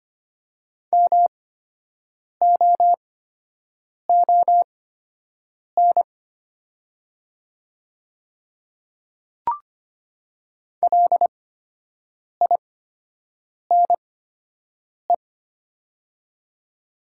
Morse code, heard as a keyed tone, about 25 wpm, spaced as for 7 wpm, 700 Hz. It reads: MOON LINE
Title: 8x Character Spacing - Top 500 Words - 25wpm